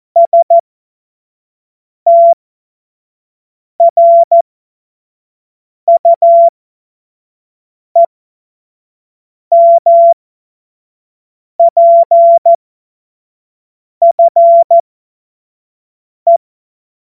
Morse code, heard as a keyed tone, about 14 wpm, spaced as for 5 wpm, 680 Hz. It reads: STRUEMPFE
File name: Morsekurs / Witze / Witz 21. Februar 2023 25 F70